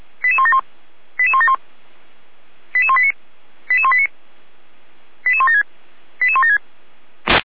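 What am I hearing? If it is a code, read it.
Five-tone selective call (ZVEI), heard as quick runs of five tones, tones of about 80 ms. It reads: 88171 88171 88189 88189 88176 88176
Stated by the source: ZVEI five-tone selective call received via SDR.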